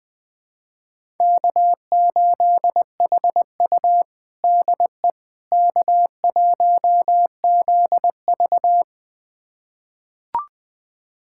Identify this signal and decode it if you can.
Morse code, heard as a keyed tone, about 20 wpm, 700 Hz. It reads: K8HU DE K1ZV
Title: USA Callsign DE Callsign - 4 Characters - 20wpm